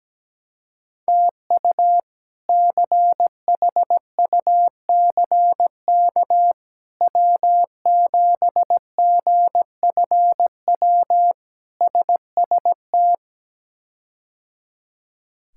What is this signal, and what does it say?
Morse code, heard as a keyed tone, about 17 wpm, 700 Hz. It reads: TU CHUCK W7GFW SST
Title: K1USN SST QSO Practice - 17wpm